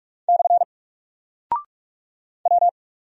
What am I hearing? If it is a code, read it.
Morse code, heard as a keyed tone, about 44 wpm, 700 Hz. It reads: / W
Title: Single Character (Letter:Number:Punctuation:Prosign) Alt 45wpm